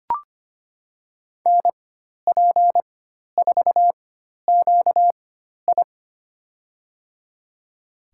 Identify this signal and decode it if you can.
Morse code, heard as a keyed tone, about 25 wpm, spaced as for 12 wpm, 700 Hz. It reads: NP4QI
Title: USA Call Signs - 5 Characters - 4x Character Spacing - 25wpm